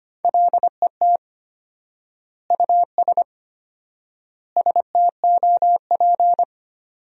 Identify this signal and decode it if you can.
Morse code, heard as a keyed tone, about 25 wpm, 700 Hz. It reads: LET US STOP
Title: 4x Word-Spacing - Sentences from Top 400 Words - 25wpm